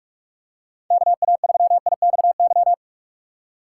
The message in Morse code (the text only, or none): KA3IXY